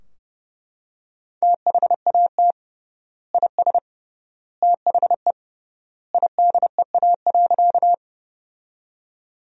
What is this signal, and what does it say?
Morse code, heard as a keyed tone, about 32 wpm, 700 Hz. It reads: THAT IS THE IDEA.
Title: 3x Word-Spacing - Sentences from Top 300 Words - 30wpm